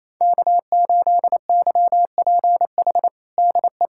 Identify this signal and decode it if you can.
Morse code, heard as a keyed tone, about 27 wpm, 700 Hz. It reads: K8YPH DE
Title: USA Callsign DE Callsign - 5 Characters - 28wpm